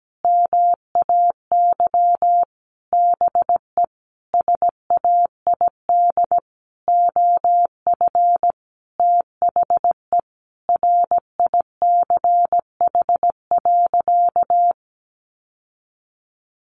Morse code, heard as a keyed tone, about 17 wpm, 700 Hz. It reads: MAY BE SAID OF THE RICH.
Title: Sentences from Top 1100 Words 17wpm